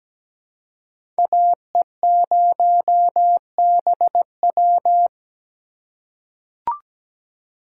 Morse code, heard as a keyed tone, about 17 wpm, 700 Hz. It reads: AE0BW
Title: USA Call Signs - 6x Character Spacing - 17wpm